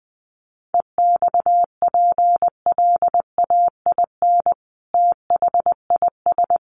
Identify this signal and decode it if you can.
Morse code, heard as a keyed tone, about 20 wpm, 700 Hz. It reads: EXPLAIN THIS
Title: Sentences from Top 1200 Words 20wpm